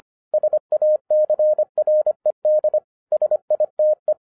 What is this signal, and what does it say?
Morse code, heard as a keyed tone, about 24 wpm, 610 Hz. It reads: SACRED SITE
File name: 2011-10-mp3 / n04-JUST MOVED 25wpm